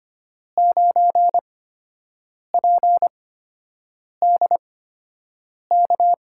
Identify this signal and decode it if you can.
Morse code, heard as a keyed tone, about 25 wpm, spaced as for 7 wpm, 700 Hz. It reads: 9PDK
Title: USA Call Signs - 5 Characters - 8x Character Spacing - 25wpm